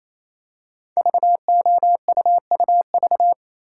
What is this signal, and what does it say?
Morse code, heard as a keyed tone, about 29 wpm, 700 Hz. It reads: VOUUV
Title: Code Groups - 5 Letters - 28wpm